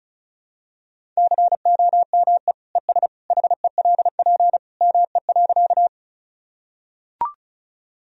Morse code, heard as a keyed tone, about 36 wpm, 700 Hz. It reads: COME ES HELP ME.
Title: Sentences from Top 200 Words and Abbreviations 35wpm